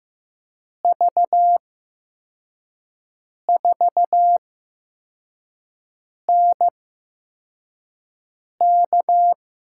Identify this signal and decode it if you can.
Morse code, heard as a keyed tone, about 15 wpm, spaced as for 4 wpm, 700 Hz. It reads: V4NK
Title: USA Call Signs - 5 Characters - 8x Character Spacing - 15wpm